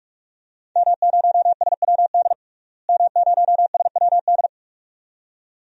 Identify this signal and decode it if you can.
Morse code, heard as a keyed tone, about 46 wpm, 700 Hz. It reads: M0SWD M0SWD
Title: DXCC-100 Call Signs - 5 Characters - Repeated 2x - 45wpm